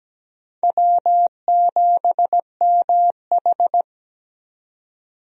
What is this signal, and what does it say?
Morse code, heard as a keyed tone, about 17 wpm, 700 Hz. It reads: W7MH
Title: USA Call Signs - 4 Characters - 17wpm